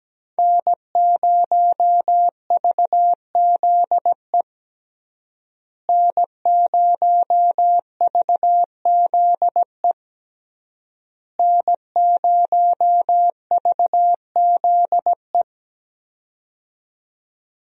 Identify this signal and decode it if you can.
Morse code, heard as a keyed tone, about 17 wpm, 700 Hz. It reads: N0VZE N0VZE N0VZE
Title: USA Call Signs - 5 Characters - Repeated 3x - 17wpm